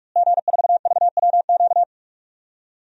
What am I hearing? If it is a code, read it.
Morse code, heard as a keyed tone, about 46 wpm, 700 Hz. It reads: G4VWQ
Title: DXCC-100 Call Signs - 5 Characters - Repeated 2x - 45wpm